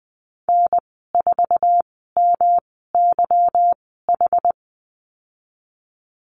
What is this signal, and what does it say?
Morse code, heard as a keyed tone, about 20 wpm, spaced as for 15 wpm, 700 Hz. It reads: N4MYH
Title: 167 Lesson 27g - Number 7 - Call Signs 20wpm